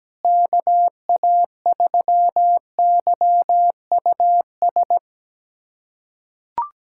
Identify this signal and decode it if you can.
Morse code, heard as a keyed tone, about 17 wpm, 700 Hz. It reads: KA3YUS
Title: USA Callsign DE Callsign - 6 Characters - 17wpm